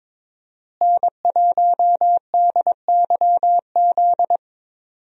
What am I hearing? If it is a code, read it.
Morse code, heard as a keyed tone, about 22 wpm, 700 Hz. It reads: N1DYZ
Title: USA Call Signs - 5 Characters - Repeated 3x - 22wpm